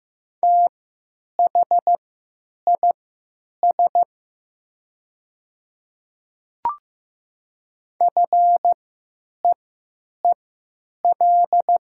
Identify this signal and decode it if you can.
Morse code, heard as a keyed tone, about 15 wpm, spaced as for 9 wpm, 700 Hz. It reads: THIS FEEL
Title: 3x Character Spacing - Top 500 Words - 15wpm